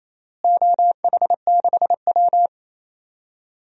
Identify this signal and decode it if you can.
Morse code, heard as a keyed tone, about 28 wpm, 700 Hz. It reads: OH6W
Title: DXCC-100 Call Signs - 4 Characters - Repeated 3x - 28wpm